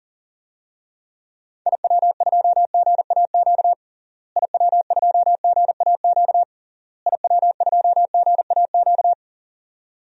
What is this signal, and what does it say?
Morse code, heard as a keyed tone, about 37 wpm, 700 Hz. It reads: IW2GAQ IW2GAQ IW2GAQ
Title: DXCC-100 Call Signs - 6 Characters - Repeated 3x - 40wpm